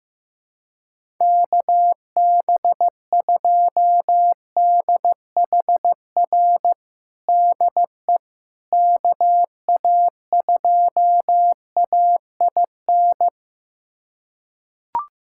Morse code, heard as a keyed tone, about 15 wpm, 700 Hz. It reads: KB2DHR DE KA2AIN
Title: USA Callsign DE Callsign - 6 Characters - 15wpm